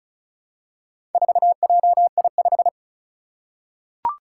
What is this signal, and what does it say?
Morse code, heard as a keyed tone, about 36 wpm, 700 Hz. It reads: 4JI5
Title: Four Challenging Letter-Number Combinations 35wpm